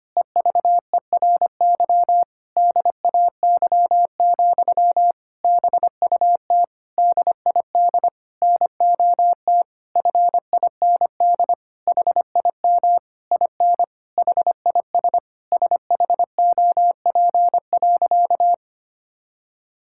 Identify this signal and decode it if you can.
Morse code, heard as a keyed tone, about 25 wpm, 700 Hz. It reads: EVERY DAY, BUT DID NOT FIND HIM IN HIS SHOP.
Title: Sentences from Top 1600 Words 25wpm